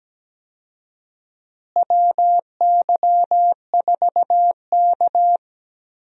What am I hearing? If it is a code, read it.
Morse code, heard as a keyed tone, about 17 wpm, 700 Hz. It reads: WY4K
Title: USA Call Signs - 4 Characters - Repeated 2x - 17wpm